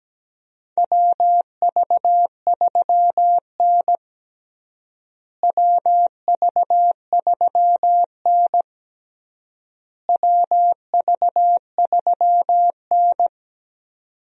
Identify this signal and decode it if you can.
Morse code, heard as a keyed tone, about 17 wpm, 700 Hz. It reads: WV3N WV3N WV3N
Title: USA Call Signs - 4 Characters - Repeated 3x - 17wpm